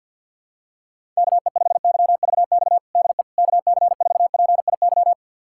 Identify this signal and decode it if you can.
Morse code, heard as a keyed tone, about 52 wpm, 700 Hz. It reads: KE5YVX DE KC4PIY
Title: USA Callsign DE Callsign - 6 Characters - 50wpm